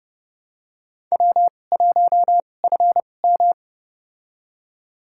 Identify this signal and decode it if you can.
Morse code, heard as a keyed tone, about 29 wpm, spaced as for 22 wpm, 700 Hz. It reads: W1FM
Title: USA Call Signs - 2x Character Spacing - 30wpm